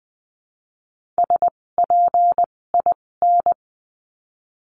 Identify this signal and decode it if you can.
Morse code, heard as a keyed tone, about 20 wpm, spaced as for 16 wpm, 700 Hz. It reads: SPIN
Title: 256 Lesson 42a - Words 12-20wpm